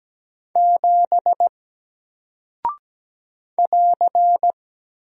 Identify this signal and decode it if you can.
Morse code, heard as a keyed tone, about 17 wpm, 700 Hz. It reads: 7 <AR>
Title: Single Character (Letter:Number:Punctuation:Prosign) Alt 17wpm